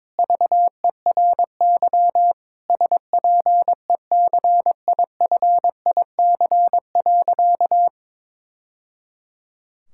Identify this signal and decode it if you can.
Morse code, heard as a keyed tone, about 22 wpm, 700 Hz. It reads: VERY SPECIFIC.
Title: Sentences from Top 1000 Words - Review Part 6 - 22wpm